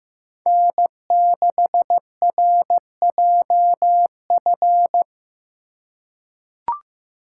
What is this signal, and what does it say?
Morse code, heard as a keyed tone, about 15 wpm, 700 Hz. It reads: N6RJF
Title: USA Call Signs - 5 Characters - 2x Character Spacing - 15wpm